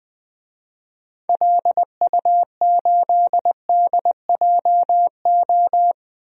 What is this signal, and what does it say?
Morse code, heard as a keyed tone, about 20 wpm, 700 Hz. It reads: LU8DJO
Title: DXCC-100 Call Signs - 6 Characters - Repeated 3x - 20wpm